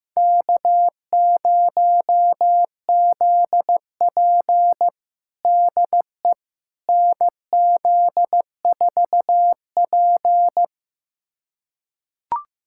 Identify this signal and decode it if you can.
Morse code, heard as a keyed tone, about 15 wpm, 700 Hz. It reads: K0ZP DE NZ4P
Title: USA Callsign DE Callsign - 4 Characters - 15wpm